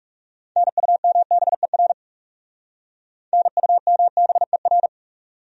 Morse code, heard as a keyed tone, about 38 wpm, 700 Hz. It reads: NUMBER NUMBER